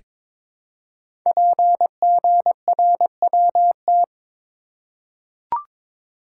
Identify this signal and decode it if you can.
Morse code, heard as a keyed tone, about 22 wpm, 700 Hz. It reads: PGRWT
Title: Code Groups - 5 Letters - 22wpm